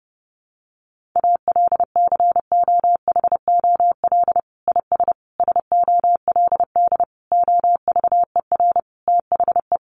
Morse code, heard as a keyed tone, about 30 wpm, 700 Hz. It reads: ALCOHOL IS SOLD OVER THE
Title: Sentences from Top 1400 Words 30wpm